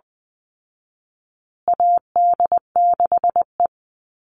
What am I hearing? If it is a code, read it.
Morse code, heard as a keyed tone, about 20 wpm, 700 Hz. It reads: AD6E
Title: Call Signs Repeated 3x - 20wpm